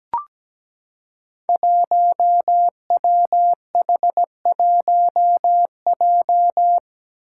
Morse code, heard as a keyed tone, about 17 wpm, 700 Hz. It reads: 1WH1J
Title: Five Challenging Letter-Number Combinations 17wpm